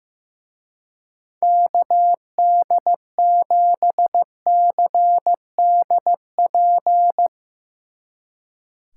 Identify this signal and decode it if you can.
Morse code, heard as a keyed tone, about 16 wpm, 700 Hz. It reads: KD7CDP
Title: Speed Racing - USA 2x3 Call Signs 15wpm